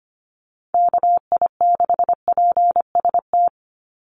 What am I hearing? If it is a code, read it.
Morse code, heard as a keyed tone, about 25 wpm, 700 Hz. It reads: KI6PST
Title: USA Call Signs 25wpm